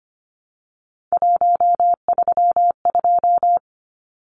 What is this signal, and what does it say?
Morse code, heard as a keyed tone, about 25 wpm, 700 Hz. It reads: 132
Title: Numbers - 3-Digit 25wpm